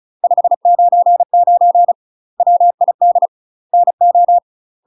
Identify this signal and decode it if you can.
Morse code, heard as a keyed tone, about 36 wpm, 700 Hz. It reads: 599 WID NO
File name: UK MASTER QSOs TRAINING / 35WPM / UK MASTER QSOs 0020